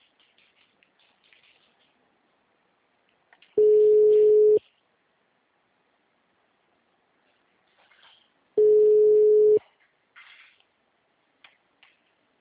Ringback tone (Europe, 425 Hz).
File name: out- -2032- - - .15394